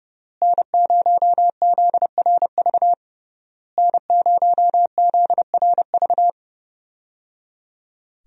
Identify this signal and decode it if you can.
Morse code, heard as a keyed tone, about 30 wpm, 700 Hz. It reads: N0ZRV N0ZRV
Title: USA Call Signs - 5 Characters - Repeated 3x - 30wpm